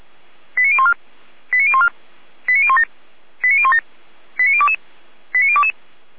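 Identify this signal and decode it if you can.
Five-tone selective call (ZVEI), heard as quick runs of five tones, tones of about 70 ms. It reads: 89014 89014 89017 89017 89022 89022
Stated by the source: Five-tone ZVEI selective call received on an SDR.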